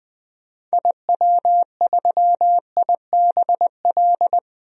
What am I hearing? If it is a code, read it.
Morse code, heard as a keyed tone, about 21 wpm, 700 Hz. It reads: IW3IBL
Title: DXCC-100 Call Signs - 6 Characters - Repeated 3x - 20wpm